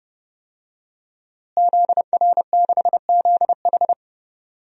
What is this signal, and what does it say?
Morse code, heard as a keyed tone, about 31 wpm, 700 Hz. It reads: ZR6ZH
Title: DXCC-100 Call Signs - 5 Characters - Repeated 3x - 30wpm